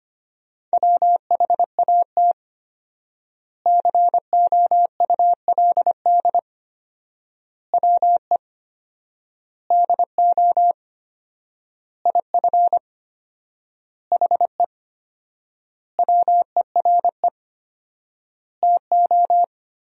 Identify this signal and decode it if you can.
Morse code, heard as a keyed tone, about 25 wpm, 700 Hz. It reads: WHAT COULD WE DO IF HE WERE TO
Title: 4x Word-Spacing - Sentences from Top 400 Words - 25wpm